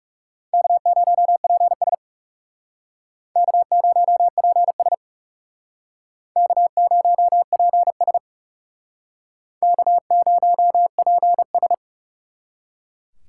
Morse code, heard as a keyed tone, about 39 wpm, 700 Hz. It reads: K0PS K0PS K0PS K0PS